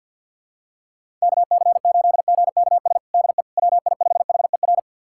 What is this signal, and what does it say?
Morse code, heard as a keyed tone, about 51 wpm, 700 Hz. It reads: KC8GKS DE WI5HER
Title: USA Callsign DE Callsign 50wpm